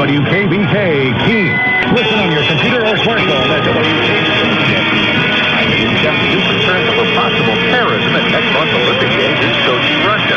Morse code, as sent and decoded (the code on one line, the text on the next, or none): none